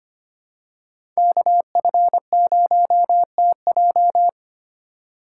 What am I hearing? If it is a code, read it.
Morse code, heard as a keyed tone, about 25 wpm, 700 Hz. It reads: KF0TJ
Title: USA Call Signs - 5 Characters - 2x Character Spacing - 25wpm